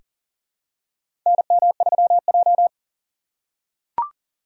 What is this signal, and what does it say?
Morse code, heard as a keyed tone, about 40 wpm, 700 Hz. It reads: NM3J
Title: USA Call Signs - 4 Characters - 4x Character Spacing - 40wpm